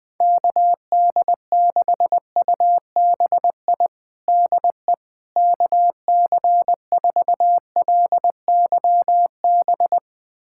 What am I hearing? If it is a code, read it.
Morse code, heard as a keyed tone, about 21 wpm, 700 Hz. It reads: KD6UBI DE KC4LYB
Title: USA Callsign DE Callsign - 6 Characters - 20wpm